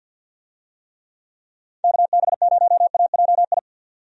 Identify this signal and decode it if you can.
Morse code, heard as a keyed tone, about 49 wpm, 700 Hz. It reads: KB0AJI
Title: USA Call Signs - 6 Characters - 50wpm